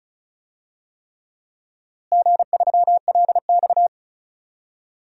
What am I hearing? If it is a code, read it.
Morse code, heard as a keyed tone, about 36 wpm, 700 Hz. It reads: G3LX
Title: DXCC-100 Call Signs - 4 Characters - Repeated 3x - 35wpm